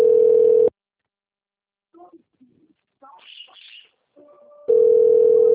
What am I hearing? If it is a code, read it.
Ringback tone (North America), 440 + 480 Hz.